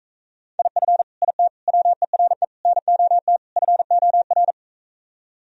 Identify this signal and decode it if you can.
Morse code, heard as a keyed tone, about 45 wpm, 700 Hz. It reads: IF IT WERE NOT FOR